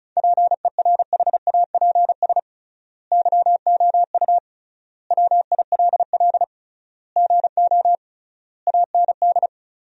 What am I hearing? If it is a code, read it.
Morse code, heard as a keyed tone, about 34 wpm, 700 Hz. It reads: PERHAPS YOU WILL GO AND
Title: 3x Word-Spacing - Sentences from Top 500 Words - 35wpm